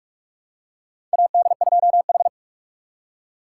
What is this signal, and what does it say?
Morse code, heard as a keyed tone, about 46 wpm, 700 Hz. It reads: AD2H
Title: USA Call Signs - 4 Characters - 3x Character Spacing - 45wpm